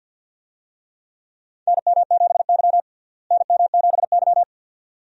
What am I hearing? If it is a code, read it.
Morse code, heard as a keyed tone, about 51 wpm, 700 Hz. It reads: NM7Y NM7Y
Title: USA Call Signs - 4 Characters - Repeated 2x - 50wpm